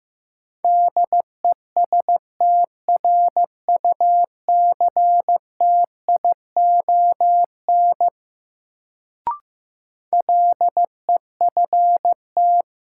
Morse code, heard as a keyed tone, about 16 wpm, 700 Hz. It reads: DESTRUCTION LEFT
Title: Sets of 5 Words Encore - New Word Drill - 15wpm